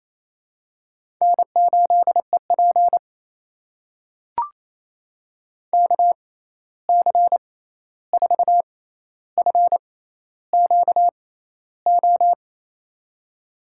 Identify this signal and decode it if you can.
Morse code, heard as a keyed tone, about 28 wpm, spaced as for 13 wpm, 700 Hz. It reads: N8EP KC4FQO
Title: USA Call Signs - 6x Character Spacing - 28wpm